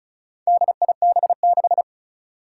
Morse code, heard as a keyed tone, about 34 wpm, 700 Hz. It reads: DIB6
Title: Four Challenging Letter-Number Combinations 35wpm